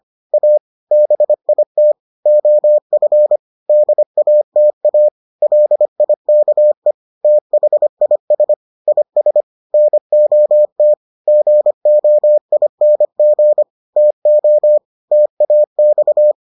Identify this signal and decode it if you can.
Morse code, heard as a keyed tone, about 25 wpm, 600 Hz. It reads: A BIT OF DATA LIKE THIS IS NOT GOING TO TAX